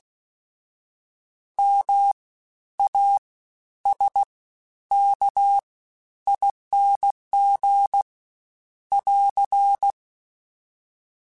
Morse code, heard as a keyed tone, about 16 wpm, spaced as for 10 wpm, 800 Hz. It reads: MASKING <AR>
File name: Special INGs-mp3s / Special INGs-mp3s / SlowToFaster ing @15